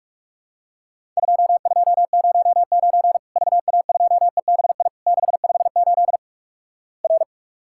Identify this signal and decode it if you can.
Morse code, heard as a keyed tone, about 46 wpm, 700 Hz. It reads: 2209 VA2EBI 658 R
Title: Worked All Europe DX Contest - QTC Exchange - 45wpm